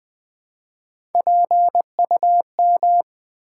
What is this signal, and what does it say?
Morse code, heard as a keyed tone, about 20 wpm, 700 Hz. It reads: PUM
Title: Build A Word - 1 Letter At A Time - 4-Letter Words - 20wpm